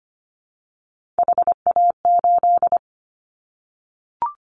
Morse code, heard as a keyed tone, about 25 wpm, 700 Hz. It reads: HA8
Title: Callsign Prefixes for Top 50 DXCC - 25wpm